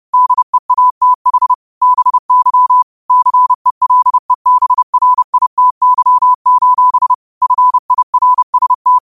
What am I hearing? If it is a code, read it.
Morse code, heard as a keyed tone, about 30 wpm, 1000 Hz. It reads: DEATH BY CELEBRITY: FIRST
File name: qotdfiles / 30wpm